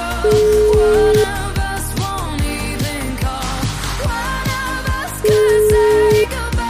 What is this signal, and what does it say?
Ringback tone (Europe, 425 Hz), with noise.